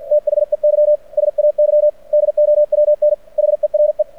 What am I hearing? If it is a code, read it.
Morse code, heard as a keyed tone, about 49 wpm, 600 Hz. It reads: THEY SAY DOWN HERE
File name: archived / CRSnet